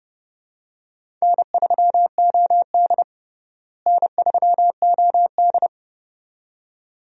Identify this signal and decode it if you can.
Morse code, heard as a keyed tone, about 32 wpm, 700 Hz. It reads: N3OD N3OD
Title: USA Call Signs - 4 Characters - Repeated 2x - 30wpm